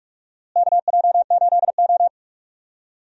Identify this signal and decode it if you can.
Morse code, heard as a keyed tone, about 46 wpm, 700 Hz. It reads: KJ8O
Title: USA Call Signs - 4 Characters - Repeated 2x - 45wpm